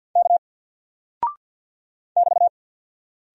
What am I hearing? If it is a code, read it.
Morse code, heard as a keyed tone, about 48 wpm, 700 Hz. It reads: K <BT>
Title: Single Character (Letter:Number:Punctuation:Prosign) Alt 50wpm